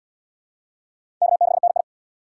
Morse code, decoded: K6NI

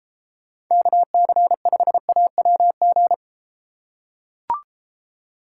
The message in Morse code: KC5AWG

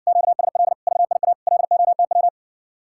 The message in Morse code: QSL VIA LOTW

